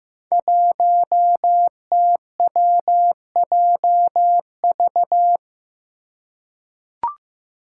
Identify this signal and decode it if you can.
Morse code, heard as a keyed tone, about 15 wpm, 700 Hz. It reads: 1TWJV
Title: Five Challenging Letter-Number Combinations 15wpm